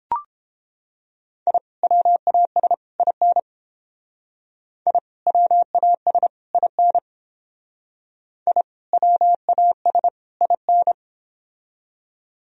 Morse code, heard as a keyed tone, about 31 wpm, 700 Hz. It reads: I WAS IN I WAS IN I WAS IN